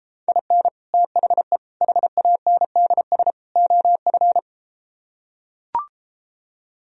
Morse code, heard as a keyed tone, about 33 wpm, 700 Hz. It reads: IN THE HANDS OF